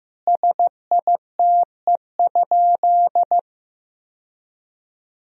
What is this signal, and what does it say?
Morse code, heard as a keyed tone, about 16 wpm, 700 Hz. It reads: SITE?
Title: Sentences from Top 1000 Words - Review Part 4 - 15wpm